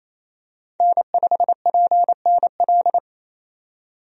Morse code, heard as a keyed tone, about 28 wpm, 700 Hz. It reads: N5PNL
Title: USA Call Signs - 5 Characters - 28wpm